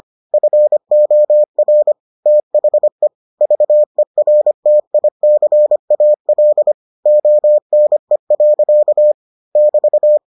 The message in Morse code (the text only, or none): FOR THE VERTICAL ONE. <BT>